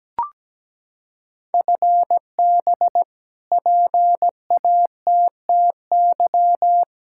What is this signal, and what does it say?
Morse code, heard as a keyed tone, about 17 wpm, 700 Hz. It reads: FB PATTY
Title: K1USN SST QSO Practice 17wpm